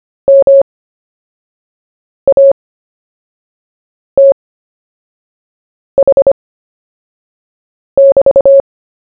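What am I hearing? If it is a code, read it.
Morse code, heard as a keyed tone, about 25 wpm, spaced as for 5 wpm, 550 Hz. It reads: MATH<BT>
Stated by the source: CW story-telling practice, Short Story 1 sent at 5 wpm, by VK4JE, Queensland, Australia.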